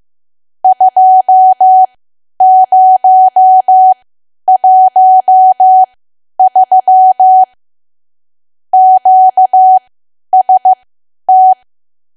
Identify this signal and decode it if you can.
Morse code, heard as a keyed tone, about 15 wpm, spaced as for 10 wpm, 750 Hz. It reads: 2013 QST
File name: cwa / W1AW-Level2 / 10WPM